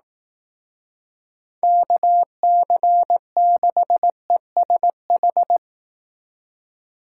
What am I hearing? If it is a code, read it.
Morse code, heard as a keyed tone, about 18 wpm, 700 Hz. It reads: KC6ESH
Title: Speed Racing - USA Call Signs 15wpm